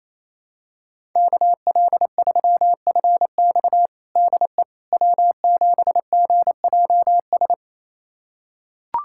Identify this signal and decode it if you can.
Morse code, heard as a keyed tone, about 28 wpm, 700 Hz. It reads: KL3FX DE W7GJS
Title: USA Callsign DE Callsign - 5 Characters - 28wpm